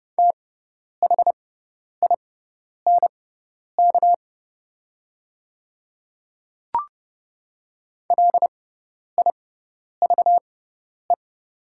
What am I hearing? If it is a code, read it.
Morse code, heard as a keyed tone, about 30 wpm, spaced as for 10 wpm, 700 Hz. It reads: THINK LIVE
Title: 6x Character Spacing - Top 300 Words - 30wpm